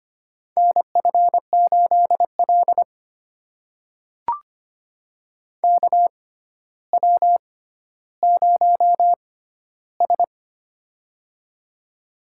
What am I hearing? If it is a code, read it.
Morse code, heard as a keyed tone, about 25 wpm, spaced as for 13 wpm, 700 Hz. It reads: NF8L KW0S
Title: USA Call Signs - 6x Character Spacing - 25wpm